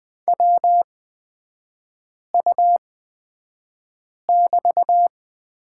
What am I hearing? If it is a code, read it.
Morse code, heard as a keyed tone, about 20 wpm, spaced as for 5 wpm, 700 Hz. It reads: WU<BT>
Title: 224 Lesson 36e - BT - Three Characters 20wpm